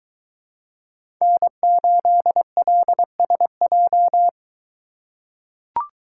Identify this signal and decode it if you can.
Morse code, heard as a keyed tone, about 22 wpm, 700 Hz. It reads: N8LSJ